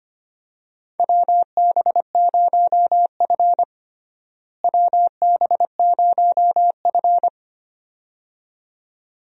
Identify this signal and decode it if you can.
Morse code, heard as a keyed tone, about 25 wpm, 700 Hz. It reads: WB0F WB0F
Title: USA Call Signs - 4 Characters - Repeated 2x - 25wpm